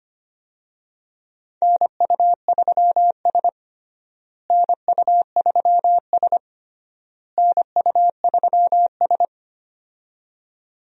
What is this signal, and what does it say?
Morse code, heard as a keyed tone, about 25 wpm, 700 Hz. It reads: NU3S NU3S NU3S
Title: USA Call Signs - 4 Characters - Repeated 3x - 25wpm